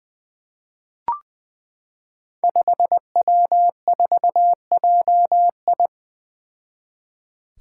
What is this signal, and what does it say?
Morse code, heard as a keyed tone, about 20 wpm, 700 Hz. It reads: 5W4JI
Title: Five Challenging Letter-Number Combinations 20wpm